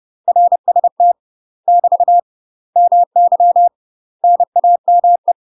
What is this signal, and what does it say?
Morse code, heard as a keyed tone, about 30 wpm, 700 Hz. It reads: RST <BT> MY NAME
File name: UK MASTER QSOs TRAINING / 30WPM / UK MASTER QSOs 0016